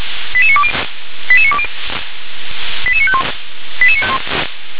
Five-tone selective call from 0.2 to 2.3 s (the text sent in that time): 88920 88920